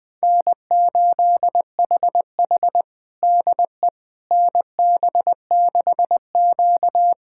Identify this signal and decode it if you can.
Morse code, heard as a keyed tone, about 20 wpm, 700 Hz. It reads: N8HH DE NB6Q
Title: USA Callsign DE Callsign - 4 Characters - 20wpm